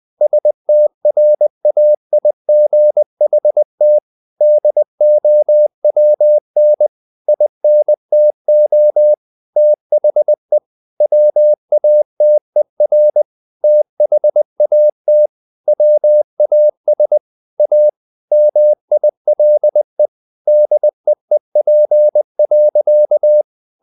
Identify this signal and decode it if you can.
Morse code, heard as a keyed tone, about 20 wpm, 600 Hz. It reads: STRAIGHT DOWN INTO THE WATER THAT WAS A MILE DEEP.
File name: the old man and the sea / the old man and the sea part 1 20wpm